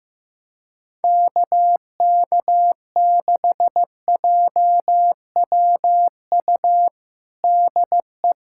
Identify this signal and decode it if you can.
Morse code, heard as a keyed tone, about 15 wpm, 700 Hz. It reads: KK6JWU DE
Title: USA Callsign DE Callsign 15wpm